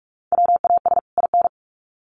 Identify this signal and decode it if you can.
Morse code, heard as a keyed tone, about 44 wpm, 700 Hz. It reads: WAS IN
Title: Sets of 2 Words 45wpm